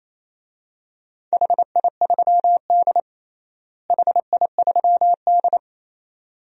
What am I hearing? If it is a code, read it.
Morse code, heard as a keyed tone, about 28 wpm, 700 Hz. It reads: HI3D HI3D
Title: DXCC-100 Call Signs - 4 Characters - Repeated 2x - 28wpm